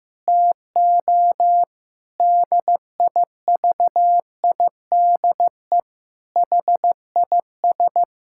Morse code, heard as a keyed tone, about 15 wpm, 700 Hz. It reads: TO DIVIDE HIS
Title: Sentences from Top 1600 Words 15wpm